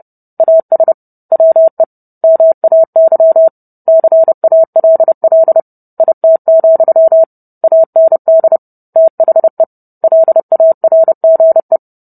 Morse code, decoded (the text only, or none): AS WE MAY CALL IT, AND THE LARGE